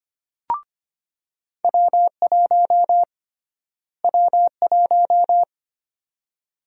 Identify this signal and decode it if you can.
Morse code, heard as a keyed tone, about 25 wpm, 700 Hz. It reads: W1 W1
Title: USA Call Sign Prefixes - Repeated 2x - 25wpm